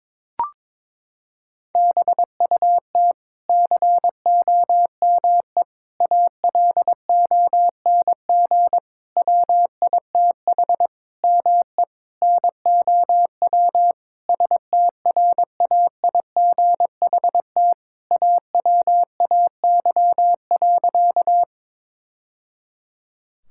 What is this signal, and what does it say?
Morse code, heard as a keyed tone, about 21 wpm, 700 Hz. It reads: BUT COME ALONG WITH ME NOW STRAIGHT AWAY.
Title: Sentences from Top 1000 Words - Review Part 6 - 22wpm